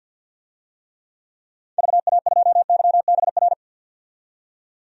Morse code, heard as a keyed tone, about 53 wpm, 700 Hz. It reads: VA2YBR
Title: DXCC-100 Call Signs - 6 Characters - 50wpm